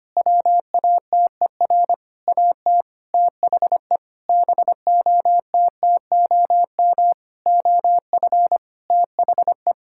Morse code, heard as a keyed tone, about 25 wpm, 700 Hz. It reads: WATER AT THE BOTTOM OF THE
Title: Sentences from Top 1000 Words - Review Part 6 - 25wpm